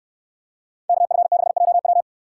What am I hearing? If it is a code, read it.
Morse code, heard as a keyed tone, about 68 wpm, 700 Hz. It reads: KF6JW